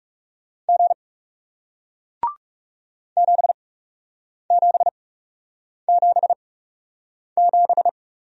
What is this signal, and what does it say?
Morse code, heard as a keyed tone, about 40 wpm, spaced as for 8 wpm, 700 Hz. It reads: G 7777